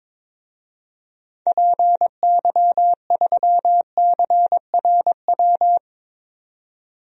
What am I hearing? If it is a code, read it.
Morse code, heard as a keyed tone, about 22 wpm, 700 Hz. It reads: PY3CRW
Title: DXCC-100 Call Signs - 6 Characters - 22wpm